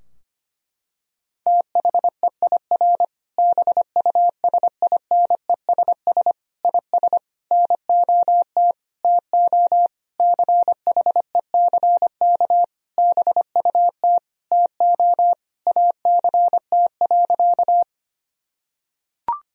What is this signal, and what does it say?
Morse code, heard as a keyed tone, about 25 wpm, 700 Hz. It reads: THEIR BUSINESS IS NOT TO CHECK BUT TO ACT.
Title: Sentences from Top 1000 Words - Review Part 3 - 25wpm